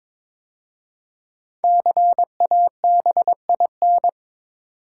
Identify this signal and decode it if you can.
Morse code, heard as a keyed tone, about 22 wpm, 700 Hz. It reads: CABIN
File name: 5 Letter Words - First 3 Letters Given - 22wpm